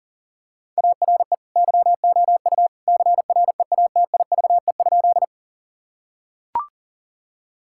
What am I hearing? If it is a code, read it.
Morse code, heard as a keyed tone, about 40 wpm, 700 Hz. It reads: ARE YOU CREATIVE?
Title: Sentences from Top 1000 Words - Review Part 5 - 40wpm